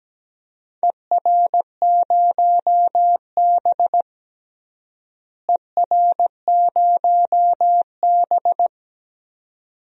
Morse code, heard as a keyed tone, about 17 wpm, 700 Hz. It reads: ER0B ER0B
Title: DXCC-100 Call Signs - 4 Characters - Repeated 2x - 17wpm